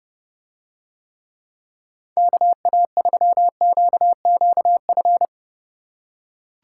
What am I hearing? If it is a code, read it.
Morse code, heard as a keyed tone, about 32 wpm, 700 Hz. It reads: KA3QQF